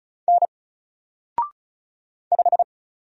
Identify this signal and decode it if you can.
Morse code, heard as a keyed tone, about 34 wpm, 700 Hz. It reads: N 5
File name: Single-Letter-Number-35wpm